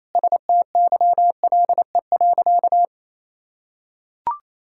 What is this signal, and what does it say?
Morse code, heard as a keyed tone, about 28 wpm, 700 Hz. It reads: STYLE.
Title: Sentences from Top 1000 Words - Review Part 5 - 28wpm